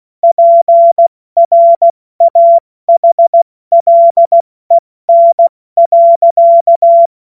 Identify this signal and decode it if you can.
Morse code, heard as a keyed tone, about 16 wpm, 680 Hz. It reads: PRAHLEN.
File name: Morsekurs / Witze / Witz 30. Dezember 2025 70 F80